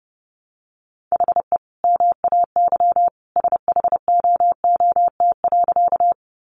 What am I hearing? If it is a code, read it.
Morse code, heard as a keyed tone, about 30 wpm, 700 Hz. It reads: HE MAY SHOOT.
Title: Sentences from Top 1200 Words 30wpm